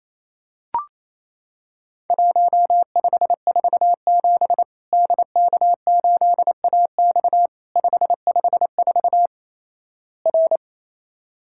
Morse code, heard as a keyed tone, about 28 wpm, 700 Hz. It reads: 1547 DK8AX 554 R
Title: Worked All Europe DX Contest - QTC Exchange - 28wpm